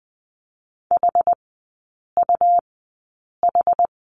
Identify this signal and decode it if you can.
Morse code, heard as a keyed tone, about 20 wpm, spaced as for 8 wpm, 700 Hz. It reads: HUH
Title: 184 Lesson 30e - Letter b - Three Characters 20wpm